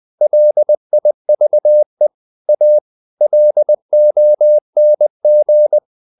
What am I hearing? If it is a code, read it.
Morse code, heard as a keyed tone, about 20 wpm, 600 Hz. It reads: LIVE A LONG
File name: the old man and the sea / the old man and the sea part 1 20wpm